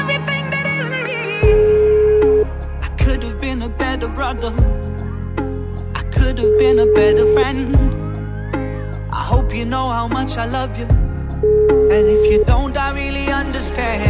Ringback tone (Europe, 425 Hz), with noise.